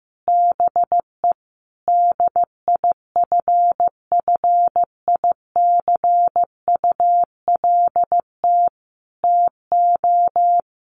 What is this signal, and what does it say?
Morse code, heard as a keyed tone, about 14 wpm, 700 Hz. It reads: BE DIFFICULT TO
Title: Sentences from Top 1400 Words 15wpm